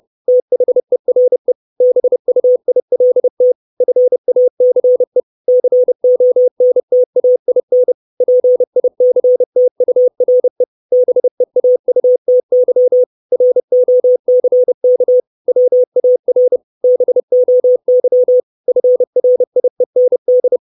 Morse code, heard as a keyed tone, about 30 wpm, 500 Hz. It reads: THERE BUILT FACE CONTAIN PICTURE BEAUTY ROCK WAR BOY FRIEND
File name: 632 Words 30WPM